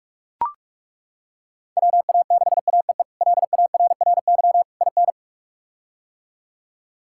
Morse code, heard as a keyed tone, about 42 wpm, 700 Hz. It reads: WA6AEE LARRY IN
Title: K1USN SST QSO Practice - 45wpm